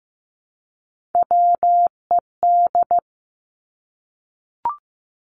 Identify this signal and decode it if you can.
Morse code, heard as a keyed tone, about 15 wpm, 700 Hz. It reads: WED
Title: Three Challenging Letter-Number Combinations 15wpm